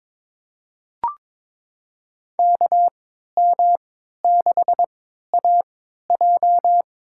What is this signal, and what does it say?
Morse code, heard as a keyed tone, about 21 wpm, spaced as for 12 wpm, 700 Hz. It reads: KM6AJ
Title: USA Call Signs - 3x Character Spacing - 22wpm